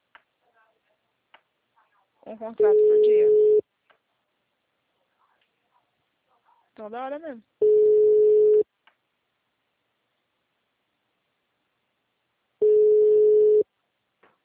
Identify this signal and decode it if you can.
Ringback tone (Europe, 425 Hz), with speech.